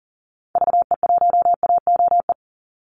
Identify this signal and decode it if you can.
Morse code, heard as a keyed tone, about 40 wpm, 700 Hz. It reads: VE1AOE
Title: Call Signs - 40wpm